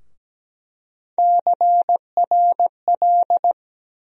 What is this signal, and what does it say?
Morse code, heard as a keyed tone, about 17 wpm, 700 Hz. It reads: CRL
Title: USA Call Sign Postfixes - 3 Letters 17wpm